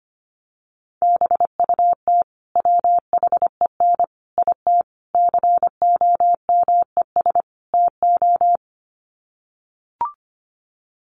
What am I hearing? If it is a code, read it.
Morse code, heard as a keyed tone, about 25 wpm, 700 Hz. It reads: BUT WHEN IT COMES TO
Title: Sets of 5 Words 25wpm